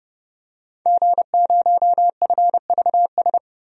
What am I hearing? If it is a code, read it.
Morse code, heard as a keyed tone, about 30 wpm, 700 Hz. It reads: G0FVS
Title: DXCC-100 Call Signs - 5 Characters - 30wpm